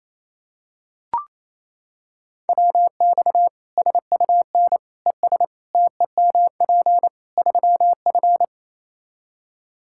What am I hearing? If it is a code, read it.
Morse code, heard as a keyed tone, about 28 wpm, 700 Hz. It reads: WX SUN ES TEMP 3F
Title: K1USN SST QSO Practice 28wpm